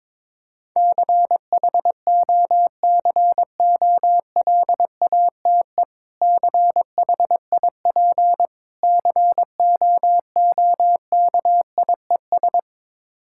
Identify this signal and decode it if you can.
Morse code, heard as a keyed tone, about 21 wpm, 700 Hz. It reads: CHOCOLATE CHIP COOKIES
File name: Difficult Words with Hints - 22wpm